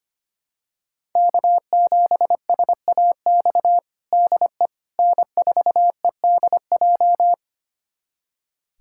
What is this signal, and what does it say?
Morse code, heard as a keyed tone, about 25 wpm, 700 Hz. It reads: K7SAX DE N4EDJ
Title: USA Callsign DE Callsign - 5 Characters - 25wpm